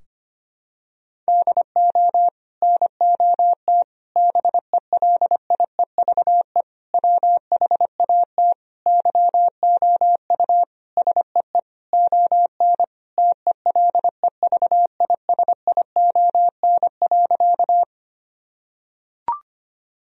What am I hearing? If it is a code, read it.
Morse code, heard as a keyed tone, about 25 wpm, 700 Hz. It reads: DO NOT BELIEVE WHAT YOU SEE ON TELEVISION.
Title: Sentences from Top 1000 Words - Review Part 4 - 25wpm